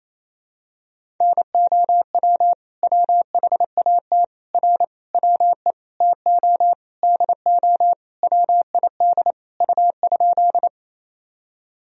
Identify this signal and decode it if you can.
Morse code, heard as a keyed tone, about 28 wpm, 700 Hz. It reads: NOW WHAT R WE TO DO WID U?
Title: Sentences from Top 100 Words and Abbreviations 28wpm